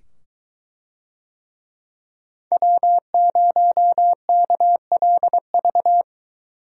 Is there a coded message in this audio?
Yes: Morse code from 2.5 to 6.0 s, heard as a keyed tone, about 21 wpm, 700 Hz. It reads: W0KLV